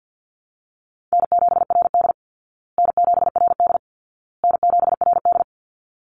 Morse code, heard as a keyed tone, about 50 wpm, 700 Hz. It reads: N7RD N7RD N7RD
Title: Easy Call Signs Repeated 3x - 50wpm